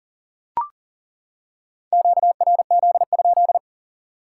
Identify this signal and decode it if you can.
Morse code, heard as a keyed tone, about 41 wpm, 700 Hz. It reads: QRZ?
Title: Top 180 QSO Elements 40wpm